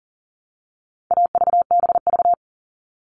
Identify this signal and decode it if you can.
Morse code, heard as a keyed tone, about 40 wpm, 700 Hz. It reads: AVBV
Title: All Cut Numbers - 4-Digit 40wpm